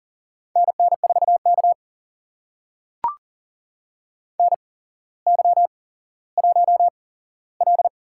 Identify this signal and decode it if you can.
Morse code, heard as a keyed tone, about 44 wpm, spaced as for 17 wpm, 690 Hz. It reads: NN4K NY1L
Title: USA Call Signs - 4 Characters - 8x Character Spacing - 40wpm